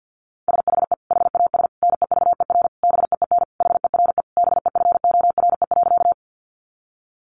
Morse code, heard as a keyed tone, about 50 wpm, 700 Hz. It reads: SHE HAS NEVER BEEN HERE BEFORE.
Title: Additional Easy Phrases 50wpm